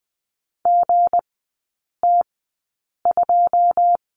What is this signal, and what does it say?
Morse code, heard as a keyed tone, about 20 wpm, spaced as for 8 wpm, 690 Hz. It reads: GT2
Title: 198 Lesson 32e - Letter k - Three Characters 20wpm